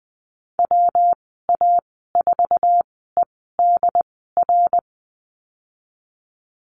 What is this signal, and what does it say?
Morse code, heard as a keyed tone, about 20 wpm, spaced as for 15 wpm, 700 Hz. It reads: WA4EDR
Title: 231 Lesson 37g - Letter x - Call Signs 20wpm